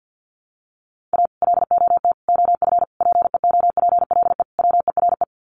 Morse code, heard as a keyed tone, about 50 wpm, 700 Hz. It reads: A LOT OF PEOPLE WERE
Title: Sets of 5 Words Encore2 50wpm